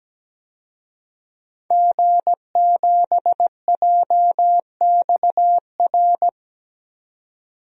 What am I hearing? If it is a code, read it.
Morse code, heard as a keyed tone, about 17 wpm, 700 Hz. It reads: G7JXR
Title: DXCC-100 Call Signs - 5 Characters - 17wpm